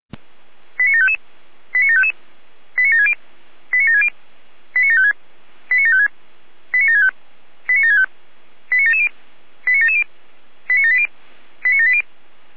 Five-tone selective call (ZVEI), heard as quick runs of five tones, tones of about 80 ms. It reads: 89755 89755 89760 89760 89756 89756 89765 89765 89770 89770 89780 89780